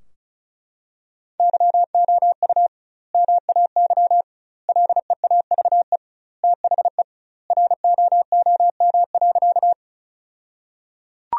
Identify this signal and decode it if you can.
Morse code, heard as a keyed tone, about 34 wpm, 700 Hz. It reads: YOU MAY LEAVE THE ROOM.
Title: 2x Word-Spacing - Sentences from Top 400 Words - 35wpm